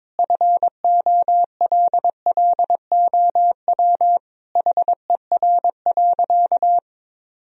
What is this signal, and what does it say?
Morse code, heard as a keyed tone, about 22 wpm, 700 Hz. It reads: FOLLOW HER.
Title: Sentences from Top 1000 Words - Review Part 1 - 22wpm